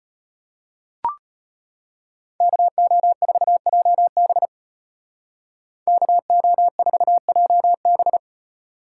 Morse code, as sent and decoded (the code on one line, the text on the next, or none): -.- --- ....- .--- -... / -.- --- ....- .--- -...
KO4JB KO4JB